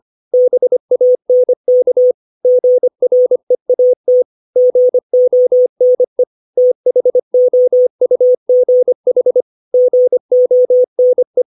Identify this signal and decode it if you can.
Morse code, heard as a keyed tone, about 25 wpm, 500 Hz. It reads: BANK GREAT GONE THOUGH GONE
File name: 632 words / 632 Words 25WPM